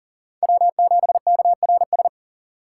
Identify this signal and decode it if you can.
Morse code, heard as a keyed tone, about 40 wpm, 700 Hz. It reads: W7KRS
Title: K1USN SST QSO Practice - 40wpm